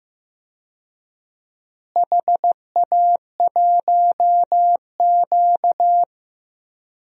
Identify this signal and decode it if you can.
Morse code, heard as a keyed tone, about 15 wpm, 700 Hz. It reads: HA1Q
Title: DXCC-100 Call Signs - 4 Characters - Repeated 3x - 15wpm